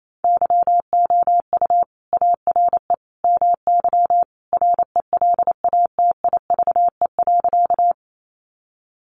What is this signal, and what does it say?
Morse code, heard as a keyed tone, about 28 wpm, 700 Hz. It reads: YOU ARE MY RELATIVE.
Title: Sentences from Top 1400 Words 28wpm